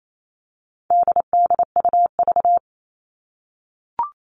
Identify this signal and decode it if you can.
Morse code, heard as a keyed tone, about 28 wpm, 700 Hz. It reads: DDUV
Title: All Cut Numbers - 4-Digit 28wpm